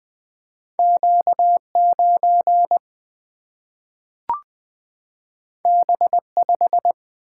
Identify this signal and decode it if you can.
Morse code, heard as a keyed tone, about 20 wpm, 700 Hz. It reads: Q 9 B 5
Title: Single-Letter Single-Number 20wpm